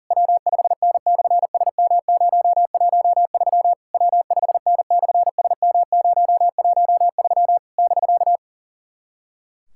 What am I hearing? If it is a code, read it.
Morse code, heard as a keyed tone, about 40 wpm, 700 Hz. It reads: W5N/SM013 W5N/SM013 <BK>
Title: SOTA QSO Practice 40wpm